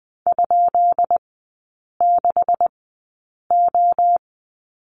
Morse code, heard as a keyed tone, about 20 wpm, spaced as for 8 wpm, 700 Hz. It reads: ?6O
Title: 152 Lesson 25e - Letter p - Three Characters 20wpm